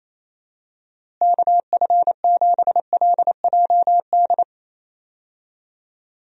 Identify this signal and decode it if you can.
Morse code, heard as a keyed tone, about 28 wpm, 700 Hz. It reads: KF7LJD
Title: USA Call Signs - 6 Characters - Repeated 2x - 28wpm